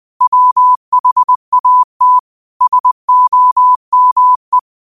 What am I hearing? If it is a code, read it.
Morse code, heard as a keyed tone, about 20 wpm, 1000 Hz. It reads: WHAT SOME